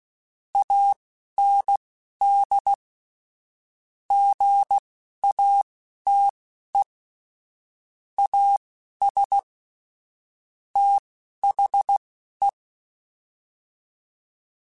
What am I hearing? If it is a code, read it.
Morse code, heard as a keyed tone, about 16 wpm, spaced as for 11 wpm, 800 Hz. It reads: AND GATE AS THE
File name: Technical Subjects / Technical Subjects-mp3s / AND Gate @10